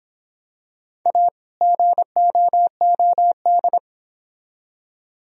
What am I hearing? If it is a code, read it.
Morse code, heard as a keyed tone, about 25 wpm, 700 Hz. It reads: A GOOD